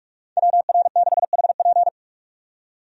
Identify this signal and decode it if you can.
Morse code, heard as a keyed tone, about 47 wpm, 700 Hz. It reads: WR6HP
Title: USA Call Signs - 5 Characters - 45wpm